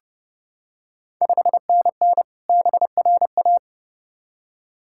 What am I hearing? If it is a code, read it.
Morse code, heard as a keyed tone, about 30 wpm, 700 Hz. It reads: 5NN BRA
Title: Ontario QSO Party Exchange 30wpm